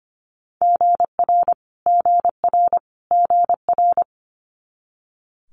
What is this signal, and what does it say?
Morse code, heard as a keyed tone, about 25 wpm, 700 Hz. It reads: GR GR GR
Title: Multi-letter Phonemes Repeated 3x 25wpm